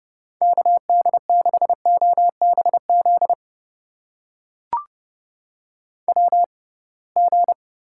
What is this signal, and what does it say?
Morse code, heard as a keyed tone, about 30 wpm, spaced as for 23 wpm, 700 Hz. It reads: KD6OBZ WG
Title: USA Call Signs - 6x Character Spacing - 30wpm